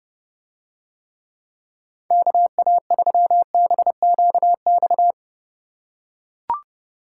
Morse code, heard as a keyed tone, about 31 wpm, 700 Hz. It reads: KA3BQX